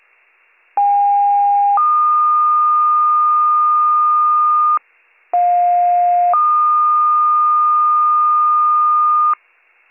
Two-tone page, heard as a one-second tone, then a three-second tone, about 810 then 1220 Hz, about 710 then 1220 Hz. It is heard from 0.8 to 9.3 s.